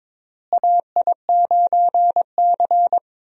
Morse code, heard as a keyed tone, about 21 wpm, 700 Hz. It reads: AI9C
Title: USA Call Signs - 4 Characters - Repeated 3x - 22wpm